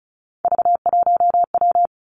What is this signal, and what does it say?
Morse code, heard as a keyed tone, about 36 wpm, 700 Hz. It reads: V1W
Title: Three Challenging Letter-Number Combinations 35wpm